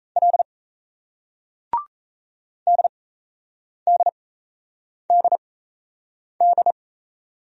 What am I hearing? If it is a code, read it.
Morse code, heard as a keyed tone, about 38 wpm, spaced as for 8 wpm, 700 Hz. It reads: L DDDD